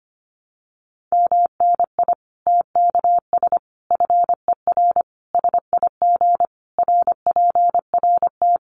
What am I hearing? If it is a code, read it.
Morse code, heard as a keyed tone, about 25 wpm, 700 Hz. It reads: MNI TKS FER SIG RPRT
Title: Common QSO Phrases 25wpm